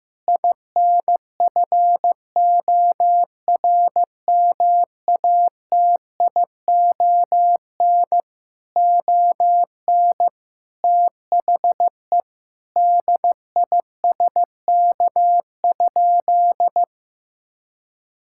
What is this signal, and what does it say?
Morse code, heard as a keyed tone, about 15 wpm, 700 Hz. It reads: INFORMATION ON THE DISK?
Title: Sentences from Top 1000 Words - Review Part 6 - 15wpm